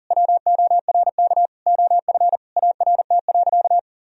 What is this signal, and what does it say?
Morse code, heard as a keyed tone, about 40 wpm, 700 Hz. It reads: WORK OF ART.
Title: Sentences from Top 1000 Words - Review Part 3 - 40wpm